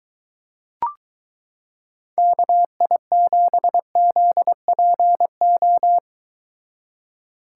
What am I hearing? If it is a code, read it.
Morse code, heard as a keyed tone, about 23 wpm, 700 Hz. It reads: KI7ZPO